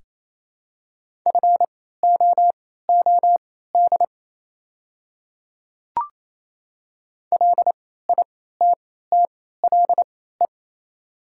Morse code, heard as a keyed tone, about 28 wpm, spaced as for 16 wpm, 700 Hz. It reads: FOOD LITTLE
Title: 3x Character Spacing - Top 300 Words - 28wpm